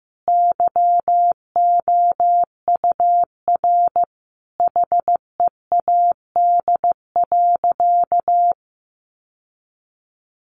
Morse code, heard as a keyed tone, about 15 wpm, 700 Hz. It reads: YOUR HEAD.
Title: Sentences from Top 1400 Words 15wpm